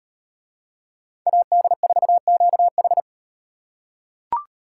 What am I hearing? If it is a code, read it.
Morse code, heard as a keyed tone, about 38 wpm, 700 Hz. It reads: AD4QH